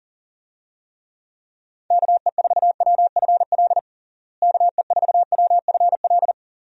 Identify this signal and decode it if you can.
Morse code, heard as a keyed tone, about 40 wpm, 700 Hz. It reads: KE4WFL KE4WFL
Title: USA Call Signs - 6 Characters - Repeated 3x - 40wpm